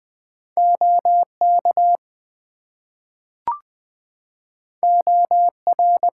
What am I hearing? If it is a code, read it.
Morse code, heard as a keyed tone, about 20 wpm, 700 Hz. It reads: OK OR
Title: Two Letter Words - 20wpm